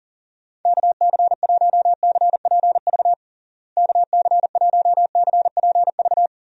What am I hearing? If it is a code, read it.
Morse code, heard as a keyed tone, about 40 wpm, 700 Hz. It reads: KC1CPV KC1CPV
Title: USA Call Signs - 6 Characters - Repeated 3x - 40wpm